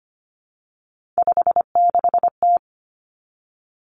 Morse code, heard as a keyed tone, about 25 wpm, 700 Hz. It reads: 56T
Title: Three Challenging Letter-Number Combinations 25wpm